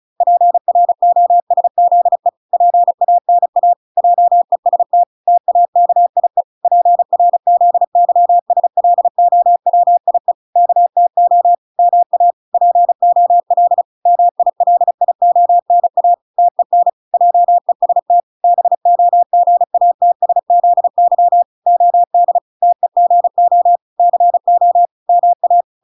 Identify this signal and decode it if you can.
Morse code, heard as a keyed tone, about 34 wpm, 700 Hz. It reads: PROSZE PANA JEST TAKIE PRZYSLOWIE KTO MA POL MILIONA TEN JEST BOGATSZY OD TEGO CO MA